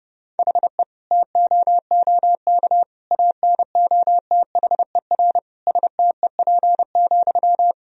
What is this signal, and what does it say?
Morse code, heard as a keyed tone, about 30 wpm, 700 Hz. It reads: HE TOOK ANOTHER STEP,
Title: Sentences from Top 1000 Words - Review Part 2 - 30wpm